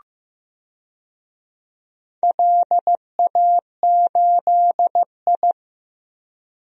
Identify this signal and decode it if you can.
Morse code, heard as a keyed tone, about 15 wpm, 700 Hz. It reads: LA8I
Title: DXCC-100 Call Signs - 4 Characters - Repeated 2x - 15wpm